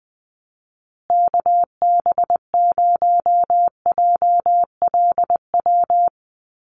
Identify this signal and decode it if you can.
Morse code, heard as a keyed tone, about 20 wpm, 700 Hz. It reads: KB0JLW
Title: USA Call Signs Repeated 2x 20wpm